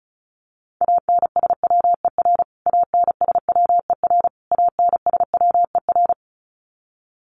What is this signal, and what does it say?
Morse code, heard as a keyed tone, about 34 wpm, 700 Hz. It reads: ANSWER ANSWER ANSWER
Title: Top 500 Words Repeated 35wpm